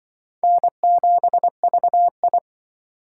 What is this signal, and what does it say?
Morse code, heard as a keyed tone, about 24 wpm, 700 Hz. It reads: N7VI
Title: Speed Racing - USA Call Signs 20wpm